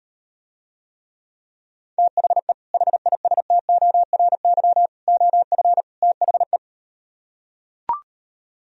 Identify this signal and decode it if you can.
Morse code, heard as a keyed tone, about 38 wpm, 700 Hz. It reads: THE HISTORY OF THE